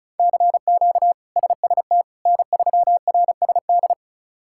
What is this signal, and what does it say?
Morse code, heard as a keyed tone, about 36 wpm, 700 Hz. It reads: CQ SST N3RSD
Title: K1USN SST QSO Practice - 35wpm